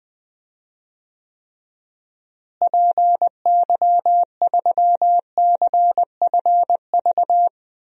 Morse code, heard as a keyed tone, about 21 wpm, 700 Hz. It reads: PY3CFV
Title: DXCC-100 Call Signs - 6 Characters - Repeated 2x - 20wpm